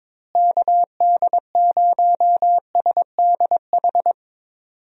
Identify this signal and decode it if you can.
Morse code, heard as a keyed tone, about 22 wpm, 700 Hz. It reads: KD0SDH
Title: USA Call Signs - 6 Characters - Repeated 3x - 22wpm